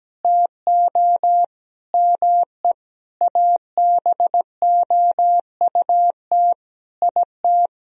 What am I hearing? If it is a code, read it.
Morse code, heard as a keyed tone, about 17 wpm, 700 Hz. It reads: TO ME ABOUT IT
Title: Sentences from Top 1000 Words - Review Part 4 - 17wpm